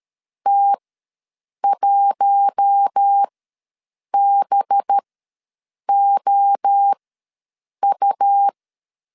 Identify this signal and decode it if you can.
Morse code, heard as a keyed tone, about 13 wpm, spaced as for 7 wpm, 800 Hz. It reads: T1BOU
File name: LEZIONI / 6 PKX0 / 6 PKX0 ws 12wpm 3448